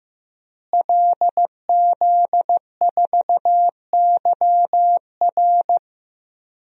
Morse code, heard as a keyed tone, about 15 wpm, 700 Hz. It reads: LZ4YR
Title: DXCC-100 Call Signs - 5 Characters - Repeated 3x - 15wpm